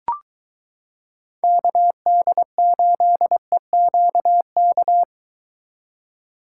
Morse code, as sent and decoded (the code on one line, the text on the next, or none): -.- -.. ---.. --.- -.-
KD8EQK